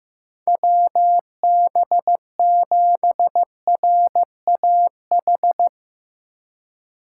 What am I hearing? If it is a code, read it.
Morse code, heard as a keyed tone, about 16 wpm, 700 Hz. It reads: WB7RAH
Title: Speed Racing - USA 2x3 Call Signs 15wpm